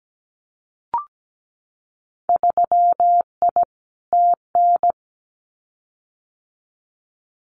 Morse code, heard as a keyed tone, about 17 wpm, 690 Hz. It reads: 3I TN
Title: Winter Field Day Exchanges 17wpm